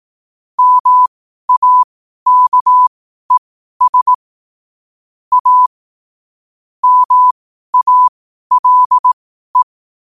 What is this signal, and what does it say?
Morse code, heard as a keyed tone, about 18 wpm, spaced as for 12 wpm, 1000 Hz. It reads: MAKES A MALE
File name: qotdfiles / 12wpm